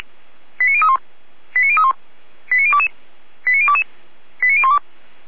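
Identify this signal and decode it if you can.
Five-tone selective call (ZVEI), heard as quick runs of five tones, tones of about 70 ms. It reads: 89031 89031 89022 89022 89012
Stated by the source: ZVEI five-tone call 89014 received by SDR.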